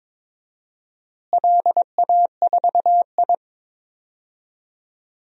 Morse code, heard as a keyed tone, about 22 wpm, 700 Hz. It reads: LA4I
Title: DXCC-100 Call Signs - 4 Characters - Repeated 3x - 22wpm